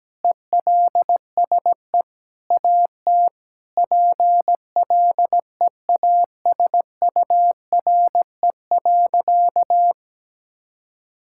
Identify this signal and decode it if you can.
Morse code, heard as a keyed tone, about 17 wpm, 700 Hz. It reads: ELSE AT PLEASURE.
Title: Sentences from Top 1600 Words 17wpm